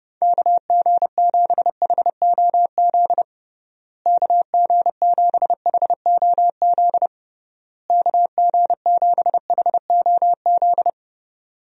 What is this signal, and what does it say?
Morse code, heard as a keyed tone, about 30 wpm, 700 Hz. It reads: KG7HOZ KG7HOZ KG7HOZ
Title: USA Call Signs - 6 Characters - Repeated 3x - 30wpm